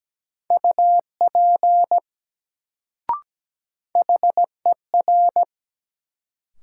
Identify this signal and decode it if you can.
Morse code, heard as a keyed tone, about 17 wpm, 700 Hz. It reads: UP HER
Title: Sets of 2 Words - New Word Drill - 17wpm